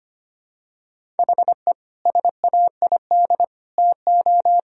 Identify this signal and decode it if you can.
Morse code, heard as a keyed tone, about 25 wpm, 700 Hz. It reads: HE SAID TO
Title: Sentences from Top 1000 Words - Review Part 1 - 25wpm